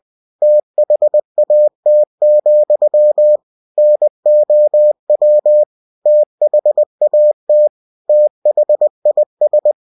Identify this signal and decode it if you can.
Morse code, heard as a keyed tone, about 20 wpm, 600 Hz. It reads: THAT, NOW THAT THIS
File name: cw-books / winnie-the-pooh / winnie-20wpm-page-0035